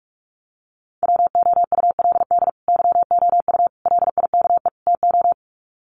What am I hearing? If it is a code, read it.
Morse code, heard as a keyed tone, about 44 wpm, 700 Hz. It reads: WOULD YOU LIKE TO
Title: Sets of 4 Words 45wpm